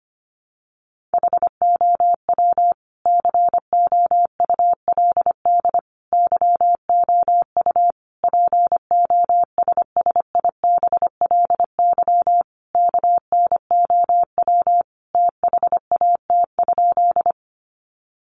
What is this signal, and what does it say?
Morse code, heard as a keyed tone, about 25 wpm, 700 Hz. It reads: HOW COULD YOU POSSIBLY KNOW THAT?
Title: Sentences from Top 1300 Words 25wpm